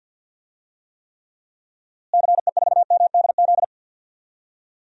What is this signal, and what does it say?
Morse code, heard as a keyed tone, about 51 wpm, 700 Hz. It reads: CE4MDZ
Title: DXCC-100 Call Signs - 6 Characters - Repeated 3x - 50wpm